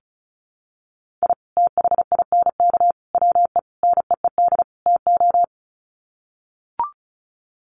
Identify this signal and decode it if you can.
Morse code, heard as a keyed tone, about 34 wpm, 700 Hz. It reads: I THINK WE NEED TO
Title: Sets of 5 Words 35wpm